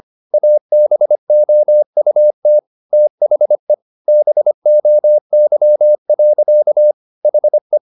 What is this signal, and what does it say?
Morse code, heard as a keyed tone, about 25 wpm, 600 Hz. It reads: ABOUT THE BOY. HE
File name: the old man and the sea part 1 25wpm